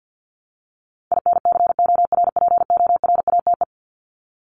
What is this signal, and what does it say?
Morse code, heard as a keyed tone, about 48 wpm, 700 Hz. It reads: INCORPORATE
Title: Top 1751-2000 Words 50wpm